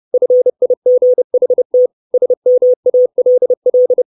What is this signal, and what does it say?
Morse code, heard as a keyed tone, about 30 wpm, 500 Hz. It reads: FIGHT SMALL
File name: 632 Words 30WPM